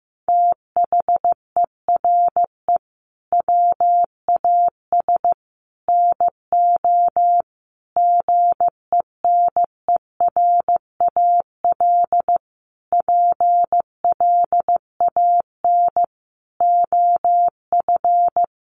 Morse code, heard as a keyed tone, about 15 wpm, 700 Hz. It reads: THERE WAS NO GENERAL PLAN OF
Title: Sentences from Top 1200 Words 15wpm